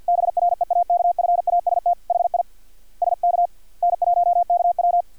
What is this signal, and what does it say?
Morse code, heard as a keyed tone, about 50 wpm, 700 Hz. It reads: BREAKFAST HI SK N1KW
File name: archived / CRSnet